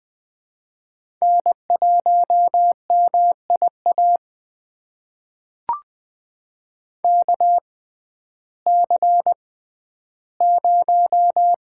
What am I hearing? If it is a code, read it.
Morse code, heard as a keyed tone, about 20 wpm, spaced as for 12 wpm, 700 Hz. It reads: N1MIA KC0
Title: USA Call Signs - 6x Character Spacing - 20wpm